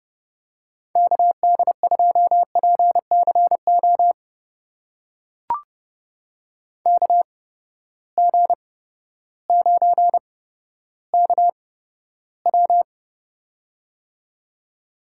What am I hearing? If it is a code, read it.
Morse code, heard as a keyed tone, about 32 wpm, spaced as for 14 wpm, 700 Hz. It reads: KD2PCO KG9KW
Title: USA Call Signs - 8x Character Spacing - 30wpm